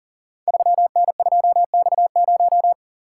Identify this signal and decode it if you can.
Morse code, heard as a keyed tone, about 40 wpm, 700 Hz. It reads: 3N2X0
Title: Code Groups - 5 Letter:Numbers - 40wpm